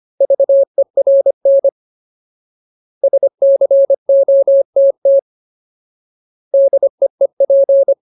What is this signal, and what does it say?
Morse code, heard as a keyed tone, about 25 wpm, 550 Hz. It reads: VERN SCOTT DEEP
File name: Ultimate Word List / 25WPM Ultimate